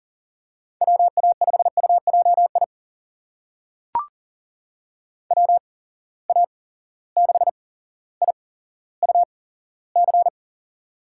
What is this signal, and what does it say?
Morse code, heard as a keyed tone, about 40 wpm, spaced as for 17 wpm, 700 Hz. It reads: WA5UJI WA6IUC
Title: USA Call Signs - 6 Characters - 8x Character Spacing - 40wpm